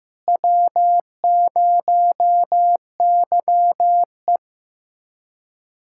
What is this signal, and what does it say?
Morse code, heard as a keyed tone, about 15 wpm, 700 Hz. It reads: W0YE
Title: USA Callsign DE Callsign - 4 Characters - 15wpm